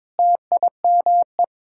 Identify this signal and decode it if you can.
Morse code, heard as a keyed tone, about 22 wpm, 700 Hz. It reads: TIME
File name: Sets of 9 Words - 22wpm